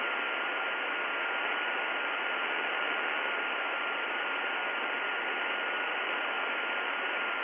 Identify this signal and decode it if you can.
FT8.
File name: contests / 2019-june=vhf-k5tr- -0101 0-20